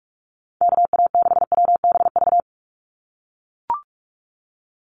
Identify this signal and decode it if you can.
Morse code, heard as a keyed tone, about 44 wpm, 700 Hz. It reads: KA6WBV
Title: USA Call Signs 45wpm